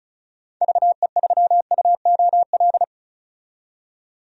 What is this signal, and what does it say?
Morse code, heard as a keyed tone, about 36 wpm, 700 Hz. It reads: VE3UOL
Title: DXCC-100 Call Signs - 6 Characters - 35wpm